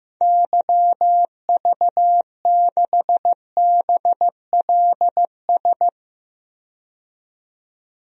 Morse code, heard as a keyed tone, about 15 wpm, 700 Hz. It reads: YV6BLS
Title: DXCC-100 Call Signs - 6 Characters - 15wpm